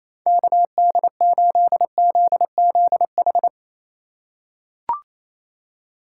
Morse code, heard as a keyed tone, about 27 wpm, 700 Hz. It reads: KD8ZZH
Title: USA Call Signs - 6 Characters - 4x Character Spacing - 28wpm